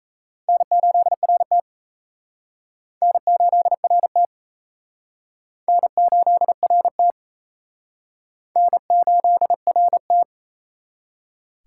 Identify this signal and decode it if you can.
Morse code, heard as a keyed tone, about 34 wpm, 700 Hz. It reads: N8RT N8RT N8RT N8RT
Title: Speed Racing - USA Call Signs 28wpm